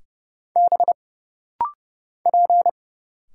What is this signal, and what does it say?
Morse code, heard as a keyed tone, about 30 wpm, 700 Hz. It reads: B P
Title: Single Letters - Rapid-Fire 30wpm